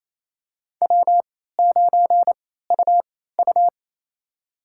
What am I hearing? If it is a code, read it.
Morse code, heard as a keyed tone, about 28 wpm, spaced as for 16 wpm, 700 Hz. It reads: W9UU
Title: USA Call Signs - 4 Characters - 3x Character Spacing - 28wpm